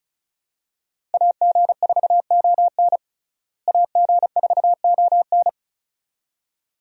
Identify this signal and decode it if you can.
Morse code, heard as a keyed tone, about 34 wpm, 700 Hz. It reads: AG4ON AG4ON
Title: USA Call Signs - 5 Characters - Repeated 2x - 35wpm